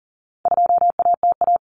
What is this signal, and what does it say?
Morse code, heard as a keyed tone, about 40 wpm, 700 Hz. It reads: 2ATA
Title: Common Cut Numbers - 4-Digit 40wpm